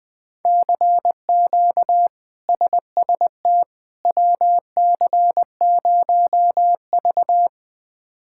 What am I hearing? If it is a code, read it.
Morse code, heard as a keyed tone, about 21 wpm, 700 Hz. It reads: CQ SST WC0V
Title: K1USN SST QSO Practice - 20wpm